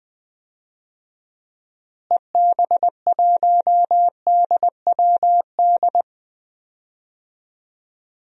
Morse code, heard as a keyed tone, about 20 wpm, 700 Hz. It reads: EB1DWD
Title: DXCC-100 Call Signs - 6 Characters - Repeated 3x - 20wpm